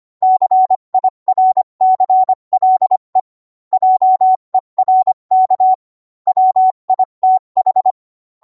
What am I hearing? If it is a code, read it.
Morse code, heard as a keyed tone, about 25 wpm, 750 Hz. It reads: CIRCLE JERK WITH